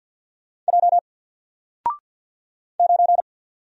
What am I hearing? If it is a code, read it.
Morse code, heard as a keyed tone, about 49 wpm, 700 Hz. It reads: J 9
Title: Single Character (Letter:Number:Punctuation:Prosign) Alt 50wpm